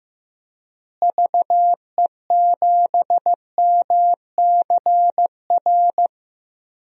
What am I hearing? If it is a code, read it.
Morse code, heard as a keyed tone, about 15 wpm, 700 Hz. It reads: VE7MCR
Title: DXCC-100 Call Signs - 6 Characters - Repeated 3x - 15wpm